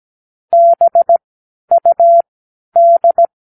Morse code, heard as a keyed tone, about 17 wpm, spaced as for 10 wpm, 680 Hz. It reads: BUD
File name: k7qo / Practicewords